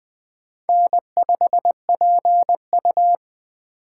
Morse code, heard as a keyed tone, about 20 wpm, 700 Hz. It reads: N5PU
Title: K1USN SST QSO Practice - 20wpm